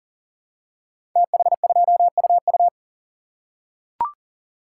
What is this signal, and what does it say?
Morse code, heard as a keyed tone, about 39 wpm, 700 Hz. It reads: TH2UU
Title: Code Groups - 5 Letter:Numbers - 40wpm